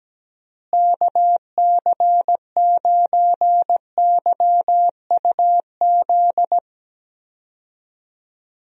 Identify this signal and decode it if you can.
Morse code, heard as a keyed tone, about 17 wpm, 700 Hz. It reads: KC9YUZ
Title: USA Call Signs - 6 Characters - Repeated 2x - 17wpm